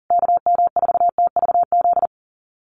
Morse code, heard as a keyed tone, about 37 wpm, 700 Hz. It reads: KM4TVZ
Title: USA Call Signs Repeated 3x 40wpm